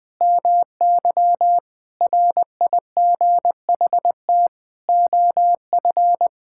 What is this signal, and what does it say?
Morse code, heard as a keyed tone, about 20 wpm, 700 Hz. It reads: MY RIGHT OF
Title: Sentences from Top 1000 Words - Review Part 6 - 20wpm